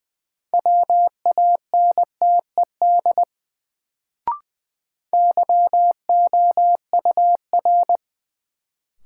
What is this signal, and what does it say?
Morse code, heard as a keyed tone, about 20 wpm, 700 Hz. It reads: WANTED YOUR
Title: Sets of 2 Words - New Word Drill - 20wpm